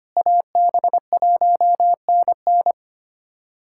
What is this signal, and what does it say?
Morse code, heard as a keyed tone, about 25 wpm, 700 Hz. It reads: AB1NN
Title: USA Callsign DE Callsign - 5 Characters - 25wpm